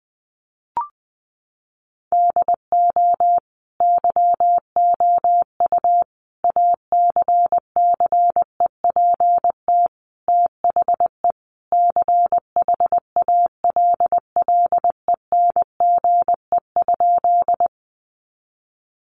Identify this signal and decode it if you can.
Morse code, heard as a keyed tone, about 20 wpm, 700 Hz. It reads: DO YOU ACCEPT THE CHALLENGE?
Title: Sentences from Top 1300 Words 20wpm